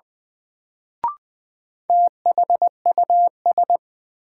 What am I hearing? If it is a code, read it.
Morse code, heard as a keyed tone, about 21 wpm, 700 Hz. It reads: THUS
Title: Sentences from Top 800 Words - New Word Drill - 20wpm